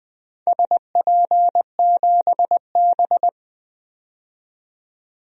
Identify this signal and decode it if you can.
Morse code, heard as a keyed tone, about 21 wpm, 700 Hz. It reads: SP7B
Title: DXCC-100 Call Signs - 4 Characters - 20wpm